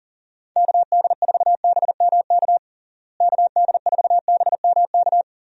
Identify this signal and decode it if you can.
Morse code, heard as a keyed tone, about 40 wpm, 700 Hz. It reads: KD4BMK KD4BMK
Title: USA Call Signs - 6 Characters - Repeated 3x - 40wpm